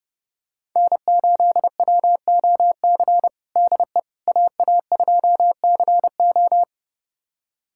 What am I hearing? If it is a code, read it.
Morse code, heard as a keyed tone, about 30 wpm, 700 Hz. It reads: N8WOC DE AA2CO
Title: USA Callsign DE Callsign - 5 Characters - 30wpm